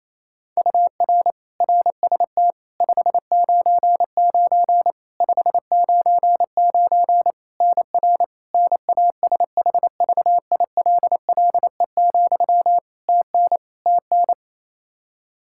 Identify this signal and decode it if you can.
Morse code, heard as a keyed tone, about 28 wpm, 700 Hz. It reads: UR RST 599 599 NR NASHVILLE, TN TN
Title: Practice QSO - Exchange 1 - Beginning - 28wpm